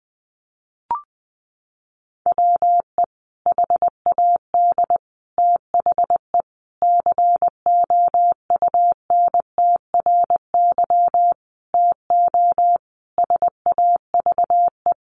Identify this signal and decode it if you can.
Morse code, heard as a keyed tone, about 20 wpm, 700 Hz. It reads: WE HAD THE COUNTRY TO SAVE
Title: Sentences from Top 1100 Words 20wpm